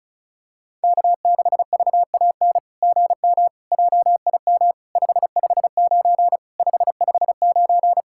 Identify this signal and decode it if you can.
Morse code, heard as a keyed tone, about 36 wpm, 700 Hz. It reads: K6VAN GM JIM 559 559
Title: SOTA QSO Practice 35wpm